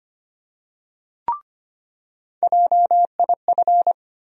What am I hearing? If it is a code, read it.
Morse code, heard as a keyed tone, about 25 wpm, 700 Hz. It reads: JIF
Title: USA Call Sign Postfixes - 3 Letters 25wpm